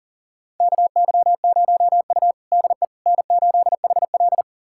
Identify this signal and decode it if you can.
Morse code, heard as a keyed tone, about 40 wpm, 700 Hz. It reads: KY0U DE N8HL
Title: USA Callsign DE Callsign - 4 Characters - 40wpm